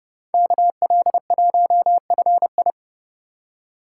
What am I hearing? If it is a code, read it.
Morse code, heard as a keyed tone, about 28 wpm, 700 Hz. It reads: KL1FI